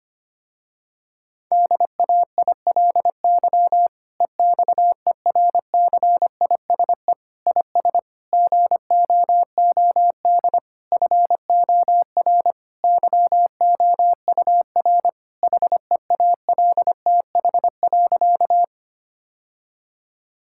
Morse code, heard as a keyed tone, about 25 wpm, 700 Hz. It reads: DAILY EXERCISE IS GOOD FOR YOUR HEALTH.
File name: practice / 25wpm / Difficult Sentences with Hints - 25wpm